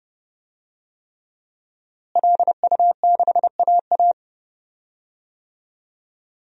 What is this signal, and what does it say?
Morse code, heard as a keyed tone, about 30 wpm, 700 Hz. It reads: LU6AA
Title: DXCC-100 Call Signs - 5 Characters - Repeated 3x - 30wpm